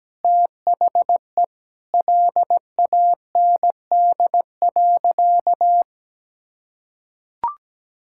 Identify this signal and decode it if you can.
Morse code, heard as a keyed tone, about 17 wpm, 700 Hz. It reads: THE LAND.
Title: Sentences from Top 1600 Words 17wpm